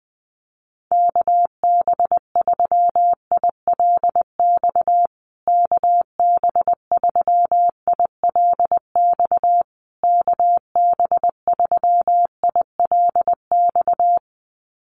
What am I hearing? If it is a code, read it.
Morse code, heard as a keyed tone, about 19 wpm, 700 Hz. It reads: KB3ILX KB3ILX KB3ILX
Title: USA Call Signs Repeated 3x 20wpm